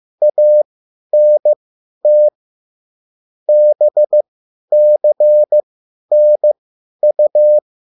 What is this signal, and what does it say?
Morse code, heard as a keyed tone, about 15 wpm, spaced as for 10 wpm, 600 Hz. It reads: ANT BCNU
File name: QSL / CW lesson